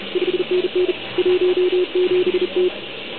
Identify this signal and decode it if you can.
Morse code, heard as a keyed tone, about 34 wpm, 360 Hz, with noise. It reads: 5NN 17T
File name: ARCHIV / OKDX2019 / OKOM40 KX3